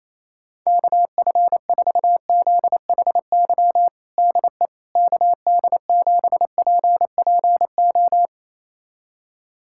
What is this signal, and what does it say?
Morse code, heard as a keyed tone, about 28 wpm, 700 Hz. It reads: KF4ZHY DE KD7PPO
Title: USA Callsign DE Callsign 28wpm